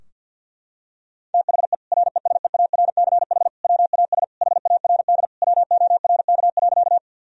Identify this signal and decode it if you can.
Morse code, heard as a keyed tone, about 52 wpm, 700 Hz. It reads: THE RESEARCH WAS HARD WORK.
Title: Sentences from Top 1000 Words - Review Part 4 - 50wpm